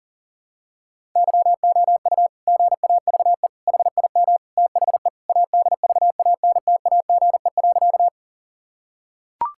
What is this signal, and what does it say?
Morse code, heard as a keyed tone, about 40 wpm, 700 Hz. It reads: YOU GAVE HIM THE ADVANTAGE.
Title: Sentences from Top 1000 Words - Review Part 6 - 40wpm